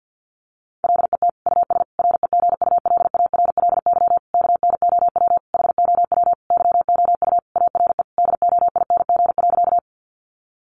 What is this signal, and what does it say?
Morse code, heard as a keyed tone, about 50 wpm, 700 Hz. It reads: LET US REGULARLY KNOW HOW YOU ARE DOING.
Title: Sentences from Top 1400 Words 50wpm